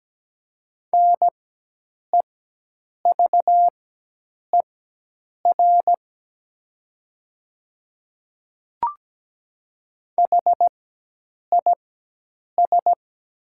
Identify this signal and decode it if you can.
Morse code, heard as a keyed tone, about 17 wpm, spaced as for 8 wpm, 700 Hz. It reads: NEVER HIS
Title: 4x Character Spacing - Top 300 Words - 17wpm